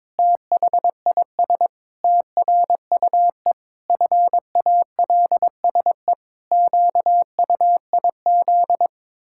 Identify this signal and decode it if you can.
Morse code, heard as a keyed tone, about 22 wpm, 700 Hz. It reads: THIS TRUE FALSE QUIZ
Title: Pangram Challenge v2 - 22wpm